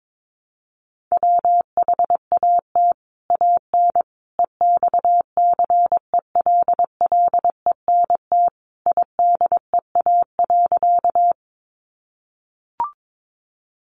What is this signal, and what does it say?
Morse code, heard as a keyed tone, about 22 wpm, 700 Hz. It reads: WHAT AN EXCELLENT IDEA.
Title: Sentences from Top 1000 Words 22wpm